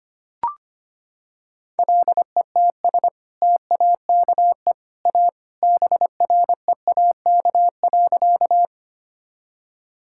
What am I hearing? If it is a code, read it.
Morse code, heard as a keyed tone, about 25 wpm, 700 Hz. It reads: LETS TAKE A BREAK.
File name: practice / 25wpm / Sentences with 2-Word Phrases - 25wpm